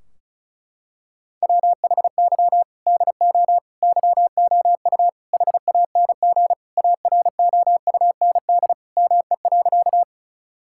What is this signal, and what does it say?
Morse code, heard as a keyed tone, about 36 wpm, 700 Hz. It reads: WHY DO YOU HANG AROUND ME.
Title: Sentences from Top 1600 Words 35wpm